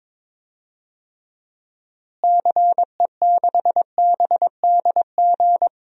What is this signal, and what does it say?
Morse code, heard as a keyed tone, about 22 wpm, 700 Hz. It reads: CE6BDG
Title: DXCC-100 Call Signs - 6 Characters - Repeated 3x - 22wpm